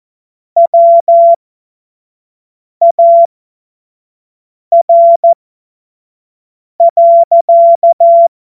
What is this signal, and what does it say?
Morse code, heard as a keyed tone, about 14 wpm, spaced as for 5 wpm, 680 Hz. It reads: WAR.
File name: Morsekurs / Witze / Witz 18. Januar 2022, 25 F70